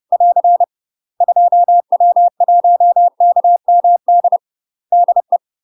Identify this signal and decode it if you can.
Morse code, heard as a keyed tone, about 30 wpm, 700 Hz. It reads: <AR> 2W1KMD DE
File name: UK MASTER QSOs TRAINING / 30WPM / UK MASTER QSOs 0039